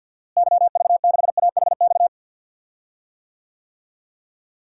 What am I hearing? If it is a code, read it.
Morse code, heard as a keyed tone, about 49 wpm, 700 Hz. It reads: YV6AHX
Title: DXCC-100 Call Signs - 6 Characters - Repeated 3x - 50wpm